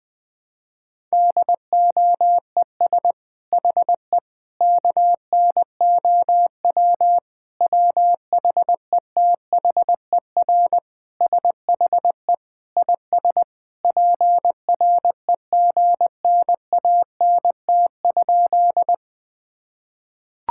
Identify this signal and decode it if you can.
Morse code, heard as a keyed tone, about 20 wpm, 700 Hz. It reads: DOES HE KNOW WHETHER SHE IS PREGNANT?
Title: Sentences from Top 1000 Words - Review Part 6 - 20wpm